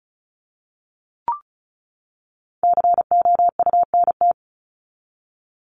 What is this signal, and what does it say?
Morse code, heard as a keyed tone, about 34 wpm, 700 Hz. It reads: COUNT
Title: Top 1251-1500 Words 35wpm